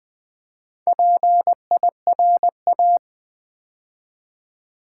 Morse code, heard as a keyed tone, about 20 wpm, 690 Hz. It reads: PIRA
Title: Build A Word - 2 Letters At A Time - 6-Letter Words - 20wpm